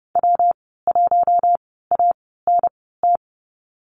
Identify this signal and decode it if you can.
Morse code, heard as a keyed tone, about 30 wpm, spaced as for 17 wpm, 700 Hz. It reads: W1ANT
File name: Lesson 19g - Letter w - Call Signs 30wpm